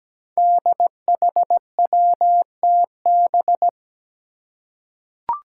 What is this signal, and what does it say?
Morse code, heard as a keyed tone, about 17 wpm, 700 Hz. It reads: DHWTB
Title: Five Challenging Letter-Number Combinations 17wpm